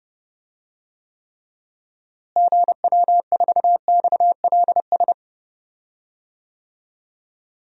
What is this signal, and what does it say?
Morse code, heard as a keyed tone, about 30 wpm, 700 Hz. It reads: GW4XLS
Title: DXCC-100 Call Signs - 6 Characters - Repeated 3x - 30wpm